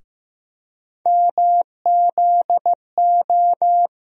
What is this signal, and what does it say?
Morse code, heard as a keyed tone, about 15 wpm, 700 Hz. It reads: MZO
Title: USA Call Sign Postfixes - 3 Letters 15wpm